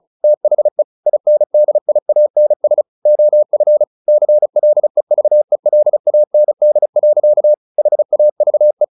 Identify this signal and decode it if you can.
Morse code, heard as a keyed tone, about 34 wpm, 600 Hz. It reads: THE INDIANS OF CLEVELAND. HAVE
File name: the old man and the sea / the old man and the sea 1 35wpm